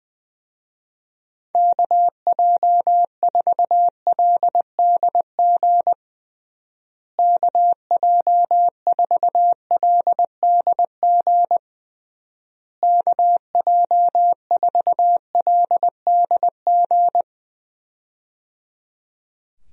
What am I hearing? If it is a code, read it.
Morse code, heard as a keyed tone, about 20 wpm, 700 Hz. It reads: KJ4LDG KJ4LDG KJ4LDG
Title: USA Call Signs - 6 Characters - Repeated 3x - 20wpm